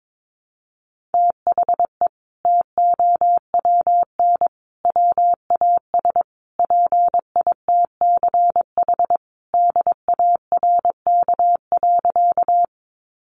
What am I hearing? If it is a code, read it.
Morse code, heard as a keyed tone, about 22 wpm, 700 Hz. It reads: THE TOWN WAS PITCH DARK.
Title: Sentences from Top 1300 Words 22wpm